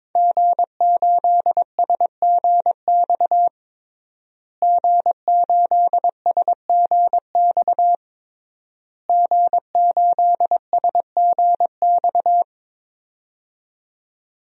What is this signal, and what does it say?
Morse code, heard as a keyed tone, about 21 wpm, 700 Hz. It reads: G8SGX G8SGX G8SGX
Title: DXCC-100 Call Signs - 5 Characters - Repeated 3x - 22wpm